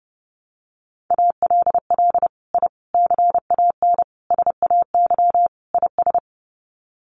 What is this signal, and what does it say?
Morse code, heard as a keyed tone, about 29 wpm, 700 Hz. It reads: ALL I CAN SAY IS
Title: Sets of 5 Words Encore2 30wpm